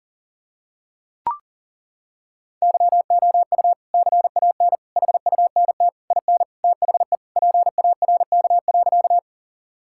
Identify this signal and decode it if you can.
Morse code, heard as a keyed tone, about 40 wpm, 700 Hz. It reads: YOU CAN HUNT IN THE PARK.
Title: Sentences from Top 1600 Words 40wpm